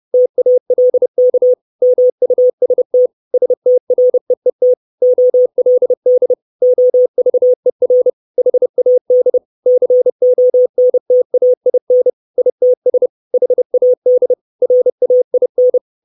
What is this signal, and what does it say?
Morse code, heard as a keyed tone, about 30 wpm, 500 Hz. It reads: TALK MUST STREET OLD OVER HAD CONTAIN ITS HAD RAIN
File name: 632 Words 30WPM